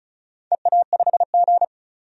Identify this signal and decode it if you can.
Morse code, heard as a keyed tone, about 36 wpm, 700 Hz. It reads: EA5G
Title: DXCC-100 Call Signs - 4 Characters - 35wpm